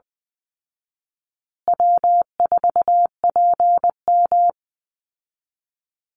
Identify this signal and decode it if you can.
Morse code, heard as a keyed tone, about 20 wpm, 700 Hz. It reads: W4PM
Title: Easy Call Signs Repeated 2x - 20wpm